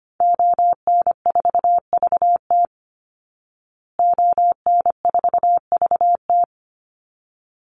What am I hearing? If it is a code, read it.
Morse code, heard as a keyed tone, about 25 wpm, 700 Hz. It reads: ON4VT ON4VT
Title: Call Signs Repeated 3x - 25wpm